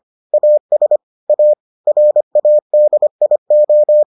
Morse code, heard as a keyed tone, about 25 wpm, 600 Hz. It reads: AS A RADIO